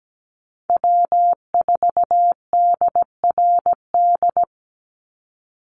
Morse code, heard as a keyed tone, about 17 wpm, 700 Hz. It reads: W4DRD
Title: USA Call Signs 17wpm